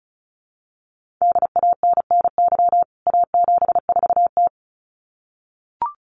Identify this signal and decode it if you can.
Morse code, heard as a keyed tone, about 34 wpm, 700 Hz. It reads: DANNY A74T
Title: CWT Exchange 35wpm